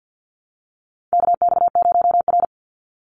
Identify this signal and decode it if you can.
Morse code, heard as a keyed tone, about 50 wpm, 700 Hz. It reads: KX0R
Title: NA SOTA Chaser Callsigns 50wpm